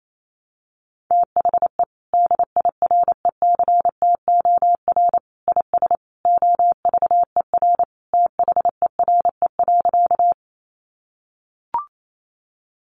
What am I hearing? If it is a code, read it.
Morse code, heard as a keyed tone, about 28 wpm, 700 Hz. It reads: THE DIRECTOR IS OVER THERE.
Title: Sentences from Top 1300 Words 28wpm